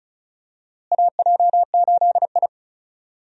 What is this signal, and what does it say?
Morse code, heard as a keyed tone, about 36 wpm, 700 Hz. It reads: AJ8I
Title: USA Call Signs - 4 Characters - Repeated 2x - 35wpm